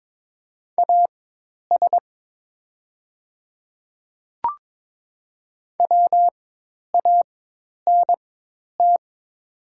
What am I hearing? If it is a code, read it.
Morse code, heard as a keyed tone, about 22 wpm, spaced as for 10 wpm, 700 Hz. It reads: AS WANT
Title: 4x Character Spacing - Top 300 Words - 22wpm